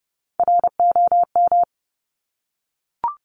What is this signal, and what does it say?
Morse code, heard as a keyed tone, about 30 wpm, 690 Hz. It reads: ROM
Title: Top Three Letter Combinations in Words 30wpm